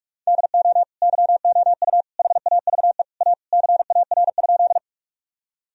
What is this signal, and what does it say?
Morse code, heard as a keyed tone, about 47 wpm, 700 Hz. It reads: DO YOU HAVE A CAR?
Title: Sentences from Top 1000 Words - Review Part 1 - 45wpm